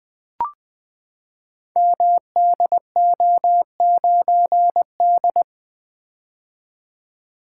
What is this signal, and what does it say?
Morse code, heard as a keyed tone, about 20 wpm, 700 Hz. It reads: MDO9D
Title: Code Groups - 5 Letter:Numbers - 20wpm